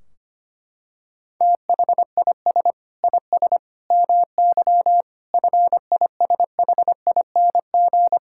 Morse code, heard as a keyed tone, about 25 wpm, 700 Hz. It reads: THIS IS MY FISHING
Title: Sentences from Top 1600 Words 25wpm